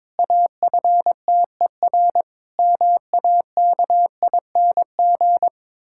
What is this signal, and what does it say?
Morse code, heard as a keyed tone, about 21 wpm, 700 Hz. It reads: AFTER MAKING
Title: Pangram Challenge v2 - 22wpm